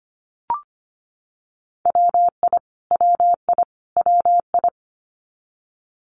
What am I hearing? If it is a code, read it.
Morse code, heard as a keyed tone, about 25 wpm, 700 Hz. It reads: WI WI WI
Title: US States Repeated 3x 25wpm